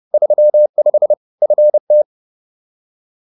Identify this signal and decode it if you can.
Morse code, heard as a keyed tone, about 30 wpm, 600 Hz. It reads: 35 FT